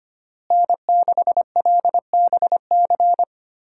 Morse code, heard as a keyed tone, about 25 wpm, 700 Hz. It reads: N6LBC
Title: USA Call Signs - 5 Characters - Repeated 3x - 25wpm